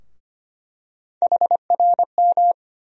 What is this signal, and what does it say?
Morse code, heard as a keyed tone, about 25 wpm, 700 Hz. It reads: HRM
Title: USA Call Sign Postfixes - 3 Letters 25wpm